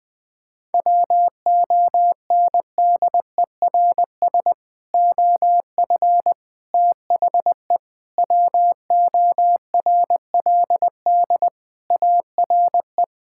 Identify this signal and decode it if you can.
Morse code, heard as a keyed tone, about 21 wpm, 700 Hz. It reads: WONDERS OF THE WORLD ARE
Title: Sentences with 3-Word Phrases - 20wpm